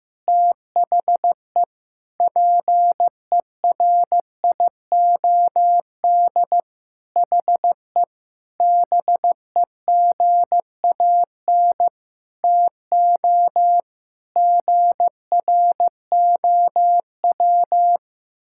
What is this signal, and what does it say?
Morse code, heard as a keyed tone, about 15 wpm, 700 Hz. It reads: THE PERIOD HE BEGAN TO GROW
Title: Sentences from Top 1000 Words - Review Part 5 - 15wpm